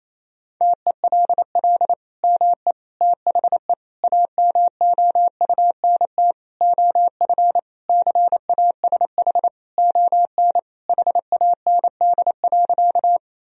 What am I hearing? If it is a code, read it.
Morse code, heard as a keyed tone, about 28 wpm, 700 Hz. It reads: TELL ME THE AMOUNT OF CASH ON HAND.
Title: Sentences from Top 1000 Words - Review Part 4 - 28wpm